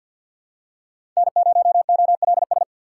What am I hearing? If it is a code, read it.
Morse code, heard as a keyed tone, about 47 wpm, 700 Hz. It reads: N0OLS
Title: USA Call Signs - 5 Characters - 6x Character Spacing - 50wpm